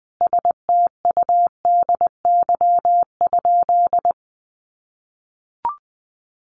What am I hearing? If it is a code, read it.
Morse code, heard as a keyed tone, about 20 wpm, 700 Hz. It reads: STUDY?
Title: Sentences from Top 1400 Words 20wpm